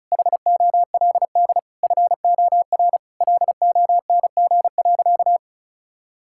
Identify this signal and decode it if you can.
Morse code, heard as a keyed tone, about 36 wpm, 700 Hz. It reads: HOLD FOR LONG.
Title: Sentences from Top 1000 Words - Review Part 5 - 35wpm